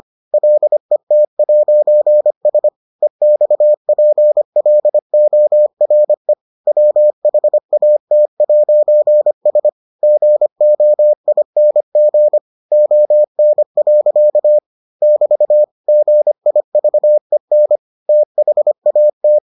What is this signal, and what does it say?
Morse code, heard as a keyed tone, about 25 wpm, 600 Hz. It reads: LET'S EXPLORE WHAT'S GOING ON. <BT> GIVEN THAT